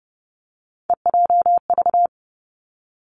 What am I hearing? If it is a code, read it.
Morse code, heard as a keyed tone, about 30 wpm, 700 Hz. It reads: EJV
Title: Three Challenging Letter-Number Combinations 30wpm